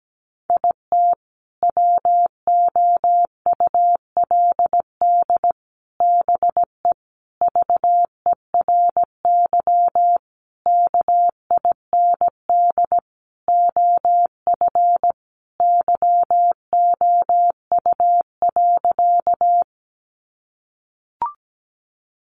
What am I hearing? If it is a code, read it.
Morse code, heard as a keyed tone, about 17 wpm, 700 Hz. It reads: IT WOULD BE VERY KIND OF YOU.
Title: Sentences from Top 200 Words 17wpm